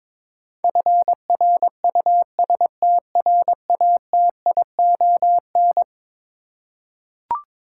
Morse code, heard as a keyed tone, about 22 wpm, 700 Hz. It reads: FRUSTRATION
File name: practice / 22wpm / Eleven Letter Words - 22wpm